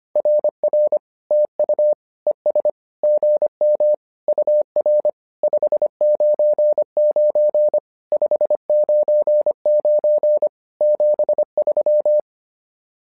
Morse code, heard as a keyed tone, about 25 wpm, 600 Hz. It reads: RR TU ES GM UR 599 599 73
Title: SOTA QSO Practice 25wpm